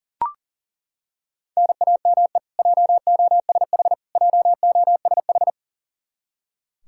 Morse code, heard as a keyed tone, about 40 wpm, 700 Hz. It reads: NAME JOSH JOSH
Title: Practice QSO - Exchange 1 - Beginning - 40wpm